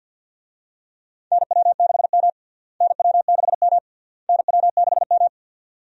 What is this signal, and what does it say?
Morse code, heard as a keyed tone, about 52 wpm, 700 Hz. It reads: NW6M NW6M NW6M
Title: USA Call Signs - 4 Characters - Repeated 3x - 50wpm